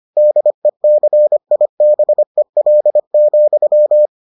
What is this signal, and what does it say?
Morse code, heard as a keyed tone, about 25 wpm, 600 Hz. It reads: DECIBEL,